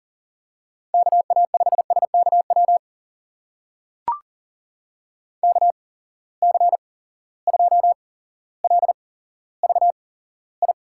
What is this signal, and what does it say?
Morse code, heard as a keyed tone, about 43 wpm, spaced as for 17 wpm, 700 Hz. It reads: KA5SKW KC2LVI
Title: USA Call Signs - 6 Characters - 8x Character Spacing - 40wpm